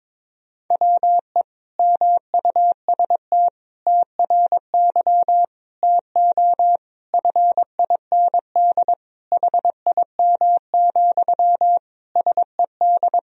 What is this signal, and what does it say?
Morse code, heard as a keyed tone, about 22 wpm, 700 Hz. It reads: WE MUST TRY TO FIND HIM, SED
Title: Sentences from Top 400 Words and Abbreviations 22wpm